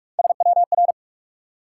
Morse code, heard as a keyed tone, about 60 wpm, 700 Hz. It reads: SWR